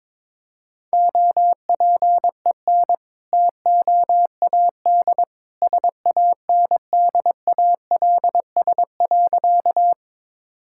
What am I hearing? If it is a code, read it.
Morse code, heard as a keyed tone, about 22 wpm, 700 Hz. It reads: OPEN TOAD SANDALS.
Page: morse-code-ninja-practice-22wpm